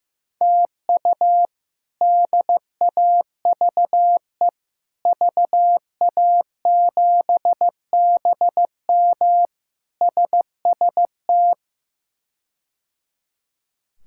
Morse code, heard as a keyed tone, about 15 wpm, 700 Hz. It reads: TU DAVE VA7BM SST
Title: K1USN SST QSO Practice - 15wpm